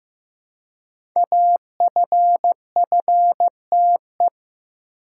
Morse code, heard as a keyed tone, about 15 wpm, 700 Hz. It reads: AFFTE
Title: Code Groups - 5 Letters - 15wpm